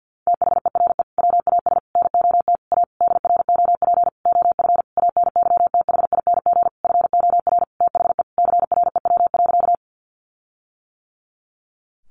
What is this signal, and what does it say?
Morse code, heard as a keyed tone, about 50 wpm, 700 Hz. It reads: THERE WAS NOT A DROP OF ANYTHING FOR THE CREW.
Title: Sentences from Top 1500 Words 50wpm